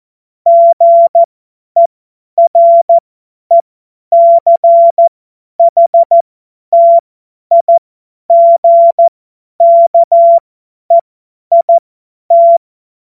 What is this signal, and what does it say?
Morse code, heard as a keyed tone, about 14 wpm, spaced as for 10 wpm, 680 Hz. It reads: GERECHTIGKEIT
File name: Morsekurs / Witze / Witz 09. August 2022 50 F70